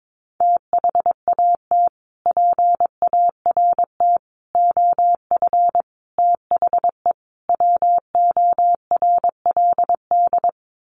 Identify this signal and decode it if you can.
Morse code, heard as a keyed tone, about 22 wpm, 700 Hz. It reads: THAT PART OF THE WORLD
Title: Sets of 5 Words Encore2 22wpm